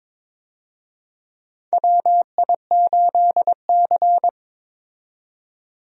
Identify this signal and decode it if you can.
Morse code, heard as a keyed tone, about 22 wpm, 700 Hz. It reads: WI8C
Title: USA Call Signs - 4 Characters - 22wpm